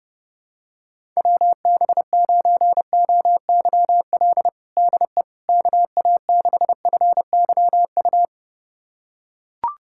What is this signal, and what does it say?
Morse code, heard as a keyed tone, about 30 wpm, 700 Hz. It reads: WB9OYL DE KA6FYU
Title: USA Callsign DE Callsign - 6 Characters - 30wpm